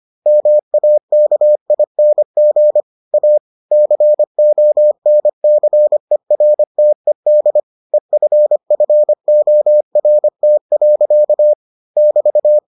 Morse code, heard as a keyed tone, about 25 wpm, 600 Hz. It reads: MAKING A CONCERTED EFFORT. <BT>